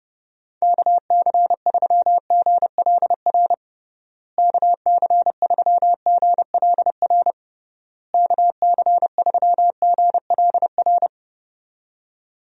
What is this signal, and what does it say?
Morse code, heard as a keyed tone, about 30 wpm, 700 Hz. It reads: KC3GLR KC3GLR KC3GLR
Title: USA Call Signs - 6 Characters - Repeated 3x - 30wpm